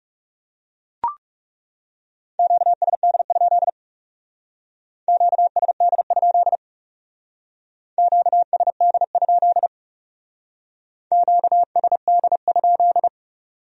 Morse code, heard as a keyed tone, about 38 wpm, 700 Hz. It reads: QSD? QSD? QSD? QSD?